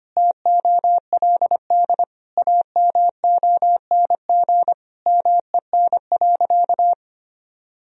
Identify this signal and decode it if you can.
Morse code, heard as a keyed tone, about 25 wpm, 700 Hz. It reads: TOLD AMONG MEN.
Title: Sentences from Top 1000 Words - Review Part 3 - 25wpm